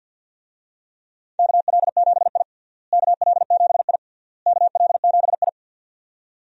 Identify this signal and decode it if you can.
Morse code, heard as a keyed tone, about 51 wpm, 700 Hz. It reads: KL7I KL7I KL7I
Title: DXCC-100 Call Signs - 4 Characters - Repeated 3x - 50wpm